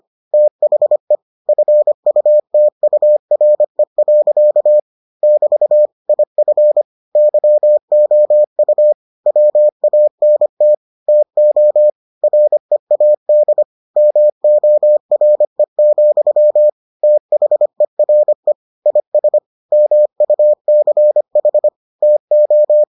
Morse code, heard as a keyed tone, about 25 wpm, 600 Hz. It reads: THE FUTURE. <BT> IF YOU WANT TO READ MORE, THERE IS MUCH TO